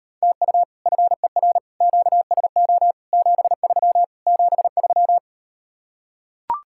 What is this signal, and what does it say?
Morse code, heard as a keyed tone, about 39 wpm, 700 Hz. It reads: TU FER QSO 73 73